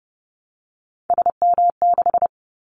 Morse code, heard as a keyed tone, about 30 wpm, 700 Hz. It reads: SM6
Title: Callsign Prefixes for Top 25 DXCC - 30wpm